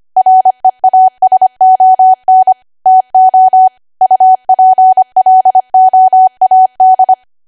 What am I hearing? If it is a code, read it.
Morse code, heard as a keyed tone, about 25 wpm, 750 Hz. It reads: REASON TO UPLOAD